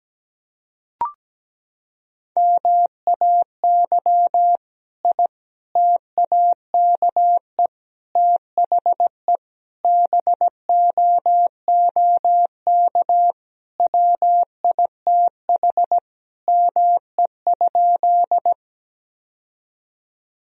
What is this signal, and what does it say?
Morse code, heard as a keyed tone, about 17 wpm, 700 Hz. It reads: MAY I TAKE THE BOOK WITH ME?
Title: Sentences from Top 1000 Words - Review Part 1 - 17wpm